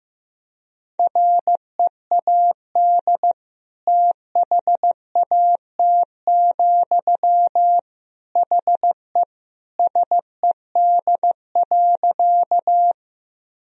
Morse code, heard as a keyed tone, about 15 wpm, 700 Hz. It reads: READ THAT, HE SED.
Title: Sentences from Top 200 Words and Abbreviations 15wpm